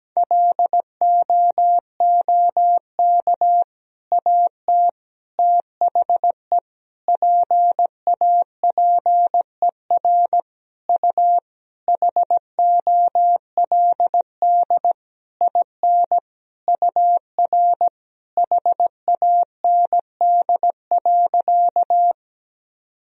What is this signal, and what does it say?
Morse code, heard as a keyed tone, about 17 wpm, 700 Hz. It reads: LOOK AT THE PAPER U HOLD IN UR HAND.
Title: Sentences from Top 400 Words and Abbreviations 17wpm